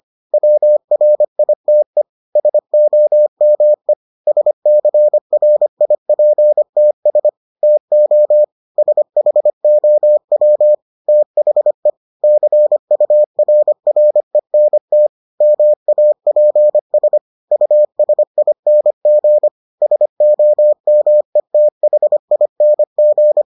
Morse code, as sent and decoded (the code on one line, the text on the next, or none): .-- .-. .. - . / ... --- -- . / ... -.-. .-. .. .--. - ... / - --- / ... .... --- .-- / - .... / -.-. ..- .-. .-. -. - / -- .- .--. ... / ..- ... .. -. --. / ... --- -- . - .... .. -. --.
WRITE SOME SCRIPTS TO SHOW THE CURRENT MAPS USING SOMETHING